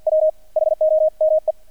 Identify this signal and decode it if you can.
Morse code, heard as a keyed tone, about 48 wpm, 630 Hz. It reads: W HOME
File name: archived / CRSnet